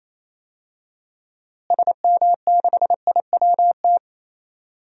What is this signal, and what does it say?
Morse code, heard as a keyed tone, about 28 wpm, 700 Hz. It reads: SM6IWT
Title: DXCC-100 Call Signs - 6 Characters - Repeated 2x - 28wpm